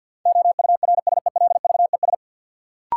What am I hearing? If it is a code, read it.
Morse code, heard as a keyed tone, about 52 wpm, 700 Hz. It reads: OURSELVES
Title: Sentences from Top 1600 Words - New Word Drill Alt - 50wpm